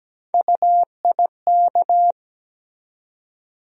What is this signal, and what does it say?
Morse code, heard as a keyed tone, about 17 wpm, 700 Hz. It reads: UIK
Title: USA Call Sign Postfixes - 3 Letters 17wpm